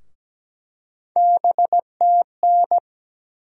Morse code, heard as a keyed tone, about 17 wpm, 700 Hz. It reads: BTN
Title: USA Call Sign Postfixes - 3 Letters 17wpm